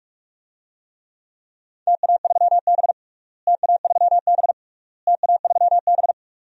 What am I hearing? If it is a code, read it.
Morse code, heard as a keyed tone, about 41 wpm, 700 Hz. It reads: TA3B TA3B TA3B
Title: DXCC-100 Call Signs - 4 Characters - Repeated 3x - 45wpm